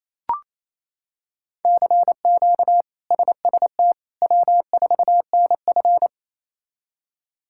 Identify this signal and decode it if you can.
Morse code, heard as a keyed tone, about 28 wpm, 700 Hz. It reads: CQ SST W4NF
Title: K1USN SST QSO Practice - 28wpm